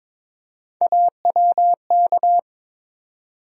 Morse code, heard as a keyed tone, about 22 wpm, 700 Hz. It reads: AWK
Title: USA Call Sign Postfixes - 3 Letters 22wpm